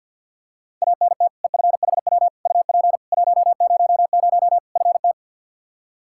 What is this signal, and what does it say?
Morse code, heard as a keyed tone, about 51 wpm, 700 Hz. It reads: ANT EFHW UP 100 FT
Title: Practice QSO - Exchange 2 - Info - 50wpm